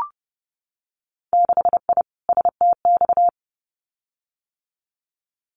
Morse code, heard as a keyed tone, about 30 wpm, 700 Hz. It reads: 6I STX
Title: Winter Field Day Exchanges Encore 30wpm